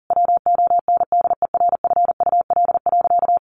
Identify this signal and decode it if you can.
Morse code, heard as a keyed tone, about 40 wpm, 700 Hz. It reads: WONDERFUL.
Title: Sentences from Top 1100 Words 40wpm